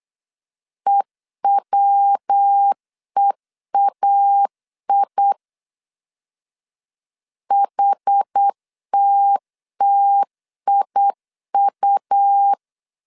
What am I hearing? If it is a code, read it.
Morse code, heard as a keyed tone, about 8 wpm, 800 Hz. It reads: EWEAI HTTIU
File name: CW / DET / 1 3 1500C S B 8wpm 4141